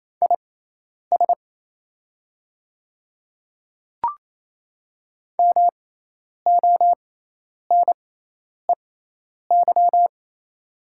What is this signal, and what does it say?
Morse code, heard as a keyed tone, about 28 wpm, spaced as for 10 wpm, 700 Hz. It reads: IS MONEY
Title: 6x Character Spacing - Top 500 Words - 28wpm